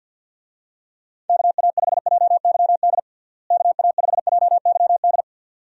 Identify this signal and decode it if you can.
Morse code, heard as a keyed tone, about 49 wpm, 700 Hz. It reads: KA5JYD KA5JYD
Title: USA Call Signs - 6 Characters - Repeated 2x - 50wpm